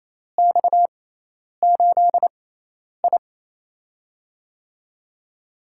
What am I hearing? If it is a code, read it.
Morse code, heard as a keyed tone, about 28 wpm, 700 Hz. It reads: X 8 I
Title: USA Call Signs - 4 Characters - 6x Character Spacing - 28wpm